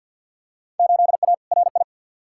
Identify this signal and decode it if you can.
Morse code, heard as a keyed tone, about 52 wpm, 700 Hz. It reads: 8A RI
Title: Field Day Exchanges Encore - 50wpm